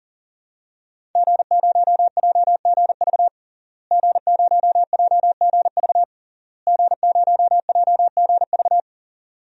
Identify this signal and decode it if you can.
Morse code, heard as a keyed tone, about 40 wpm, 700 Hz. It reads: G0JGV G0JGV G0JGV
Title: DXCC-100 Call Signs - 5 Characters - Repeated 3x - 40wpm